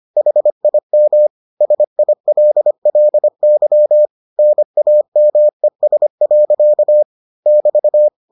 Morse code, heard as a keyed tone, about 25 wpm, 600 Hz. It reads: HIM SILLY NAMES. <BT>